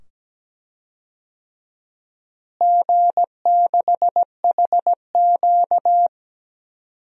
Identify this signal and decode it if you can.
Morse code, heard as a keyed tone, about 17 wpm, 700 Hz. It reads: G6HQ
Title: DXCC-100 Call Signs - 4 Characters - Repeated 2x - 17wpm